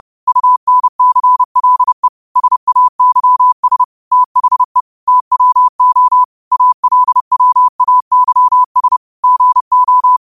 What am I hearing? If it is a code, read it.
Morse code, heard as a keyed tone, about 30 wpm, 1000 Hz. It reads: UNCLE SAYS THE TWO ALWAYS GO
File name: qotdfiles / 30wpm